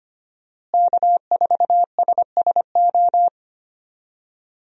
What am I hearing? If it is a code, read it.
Morse code, heard as a keyed tone, about 25 wpm, 700 Hz. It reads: K4SSO
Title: USA Call Signs - 5 Characters - Repeated 2x - 25wpm